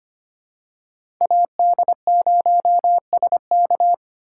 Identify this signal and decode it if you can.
Morse code, heard as a keyed tone, about 25 wpm, 700 Hz. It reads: AD0SK
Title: USA Call Signs - 2x Character Spacing - 25wpm